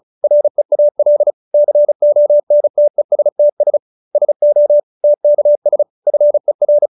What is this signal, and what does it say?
Morse code, heard as a keyed tone, about 35 wpm, 590 Hz. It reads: REAL CONTESTS SO TKS FER